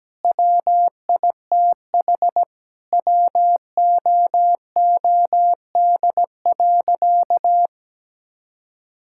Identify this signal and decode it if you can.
Morse code, heard as a keyed tone, about 17 wpm, 700 Hz. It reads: WITH WOOD.
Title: Sentences from Top 1000 Words - Review Part 6 - 17wpm